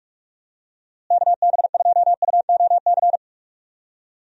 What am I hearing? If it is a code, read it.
Morse code, heard as a keyed tone, about 41 wpm, 700 Hz. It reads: KB2UOC
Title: USA Call Signs - 6 Characters - Repeated 2x - 45wpm